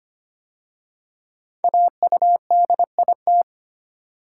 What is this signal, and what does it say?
Morse code, heard as a keyed tone, about 25 wpm, 700 Hz. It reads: AUDIT
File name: 5 Letter Words - First 3 Letters Given - 25wpm